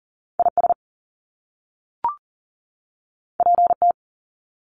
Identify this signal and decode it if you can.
Morse code, heard as a keyed tone, about 40 wpm, 700 Hz. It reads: IS PT
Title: Top Two Letter Combinations in Words 40wpm